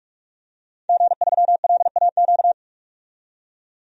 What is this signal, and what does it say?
Morse code, heard as a keyed tone, about 44 wpm, 690 Hz. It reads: G3LAQ
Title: DXCC-100 Call Signs - 5 Characters - 45wpm